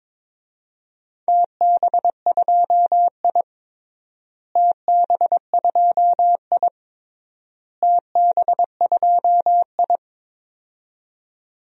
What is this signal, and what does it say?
Morse code, heard as a keyed tone, about 22 wpm, 700 Hz. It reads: TB2I TB2I TB2I
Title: DXCC-100 Call Signs - 4 Characters - Repeated 3x - 22wpm